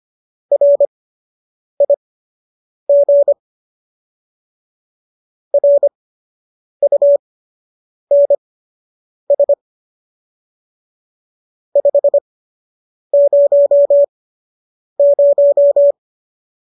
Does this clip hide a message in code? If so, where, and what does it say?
Yes: Morse code from 0.5 to 15.9 s, heard as a keyed tone, about 25 wpm, spaced as for 8 wpm, 570 Hz. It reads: RIG RUNS 500